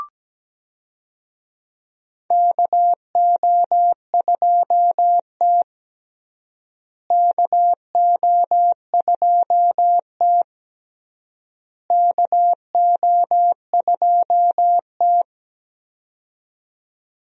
Morse code, heard as a keyed tone, about 17 wpm, 700 Hz. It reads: KO2T KO2T KO2T
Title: USA Call Signs - 4 Characters - Repeated 3x - 17wpm